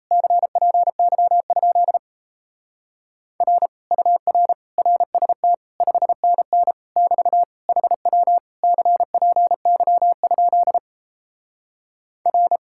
Morse code, heard as a keyed tone, about 38 wpm, 700 Hz. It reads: CPY? R UR RST 5NN <BT> HW CPY? R